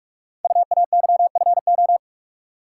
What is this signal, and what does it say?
Morse code, heard as a keyed tone, about 44 wpm, 700 Hz. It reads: UAYFO
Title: Code Groups - 5 Letters - 45wpm